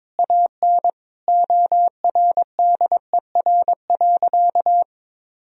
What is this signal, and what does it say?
Morse code, heard as a keyed tone, about 22 wpm, 700 Hz. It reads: AN ORDER.
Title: Sentences from Top 1000 Words - Review Part 2 - 22wpm